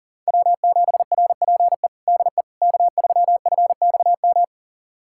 Morse code, heard as a keyed tone, about 40 wpm, 700 Hz. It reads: W7RPE DE K3FXM
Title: USA Callsign DE Callsign - 5 Characters - 40wpm